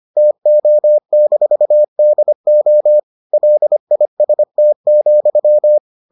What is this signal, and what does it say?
Morse code, heard as a keyed tone, about 25 wpm, 600 Hz. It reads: TO-DO LIST,